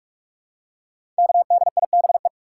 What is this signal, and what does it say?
Morse code, heard as a keyed tone, about 44 wpm, 700 Hz. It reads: KDIBE
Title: Code Groups - 5 Letters - 45wpm